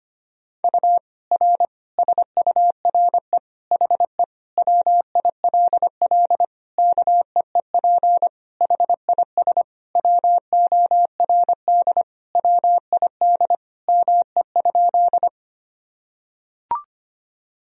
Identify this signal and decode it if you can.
Morse code, heard as a keyed tone, about 25 wpm, 700 Hz. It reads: U R SURE HE WILL KEEP HIS WORD WID ME?
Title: Sentences from Top 300 Words and Abbreviations 25wpm